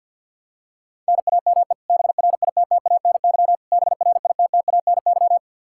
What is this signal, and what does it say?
Morse code, heard as a keyed tone, about 52 wpm, 700 Hz. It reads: NAME BRITTANY BRITTANY
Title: Practice QSO - Exchange 1 - Beginning - 50wpm